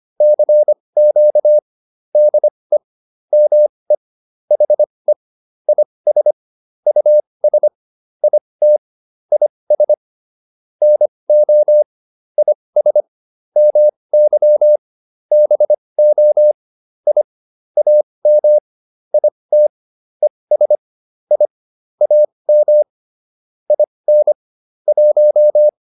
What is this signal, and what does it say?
Morse code, heard as a keyed tone, about 25 wpm, spaced as for 20 wpm, 600 Hz. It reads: CQ DE ME HE IS US IT IS NO IS MY BO I AM IT ES I AM IN 1